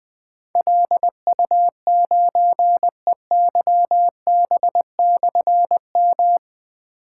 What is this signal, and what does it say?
Morse code, heard as a keyed tone, about 20 wpm, 700 Hz. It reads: LU9EYB/M
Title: Difficult Call Signs 20wpm